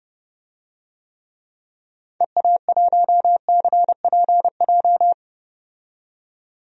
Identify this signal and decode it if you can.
Morse code, heard as a keyed tone, about 31 wpm, 700 Hz. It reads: EA1CPJ
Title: DXCC-100 Call Signs - 6 Characters - 30wpm